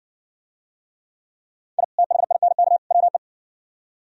Morse code, heard as a keyed tone, about 57 wpm, 700 Hz. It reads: I THINK WE